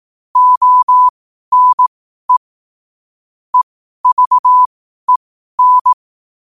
Morse code, heard as a keyed tone, about 18 wpm, spaced as for 12 wpm, 1000 Hz. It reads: ONE EVEN
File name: qotdfiles / 12wpm